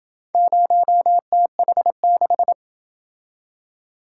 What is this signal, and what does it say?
Morse code, heard as a keyed tone, about 28 wpm, 700 Hz. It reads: 0TH6